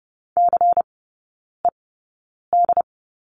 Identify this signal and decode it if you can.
Morse code, heard as a keyed tone, about 30 wpm, spaced as for 9 wpm, 700 Hz. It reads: CED
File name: Lesson 16e - Letter c - Three Characters 30wpm